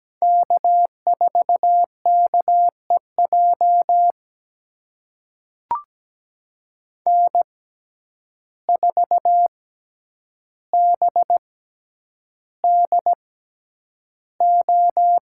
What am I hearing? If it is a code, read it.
Morse code, heard as a keyed tone, about 17 wpm, spaced as for 9 wpm, 700 Hz. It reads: K4KEJ N4BDO
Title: USA Call Signs - 5 Characters - 6x Character Spacing - 17wpm